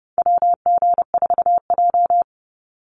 Morse code, heard as a keyed tone, about 28 wpm, 700 Hz. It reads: WG4J
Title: USA Call Signs 30wpm